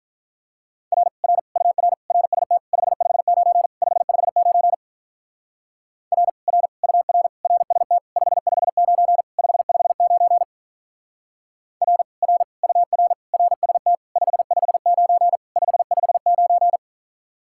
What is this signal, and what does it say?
Morse code, heard as a keyed tone, about 43 wpm, 700 Hz. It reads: R R UR RST 559 559 R R UR RST 559 559 R R UR RST 559 559
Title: Speed-Racing Common QSO Phrases - 35wpm